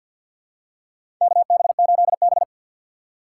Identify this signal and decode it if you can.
Morse code, heard as a keyed tone, about 52 wpm, 700 Hz. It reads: KB8B
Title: USA Call Signs - 4 Characters - 2x Character Spacing - 50wpm